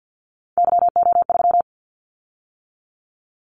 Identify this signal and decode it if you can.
Morse code, heard as a keyed tone, about 50 wpm, 700 Hz. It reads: YO3
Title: Callsign Prefixes for Top 25 DXCC - 50wpm